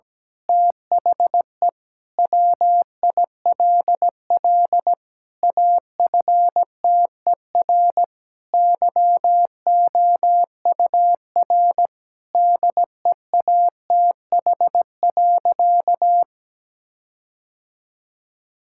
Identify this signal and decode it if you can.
Morse code, heard as a keyed tone, about 17 wpm, 700 Hz. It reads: THE WILL AFTER YOUR DEATH.
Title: Sentences from Top 1600 Words 17wpm